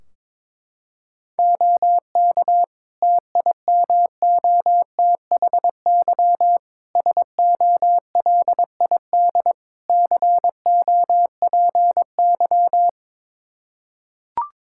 Morse code, heard as a keyed tone, about 22 wpm, 700 Hz. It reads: OK TIMOTHY SOLID COPY
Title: K1USN SST QSO Practice 22wpm